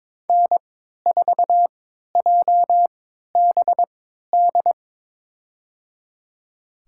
Morse code, heard as a keyed tone, about 22 wpm, spaced as for 12 wpm, 700 Hz. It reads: N4JBD
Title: USA Call Signs - 3x Character Spacing - 22wpm